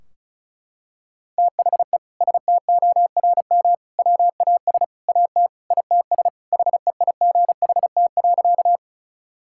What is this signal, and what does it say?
Morse code, heard as a keyed tone, about 36 wpm, 700 Hz. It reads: THE STORM WAS AT ITS HEIGHT.
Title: Sentences from Top 1600 Words 35wpm